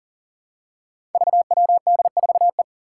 Morse code, heard as a keyed tone, about 40 wpm, 700 Hz. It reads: VWD4E
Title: Five Challenging Letter-Number Combinations 40wpm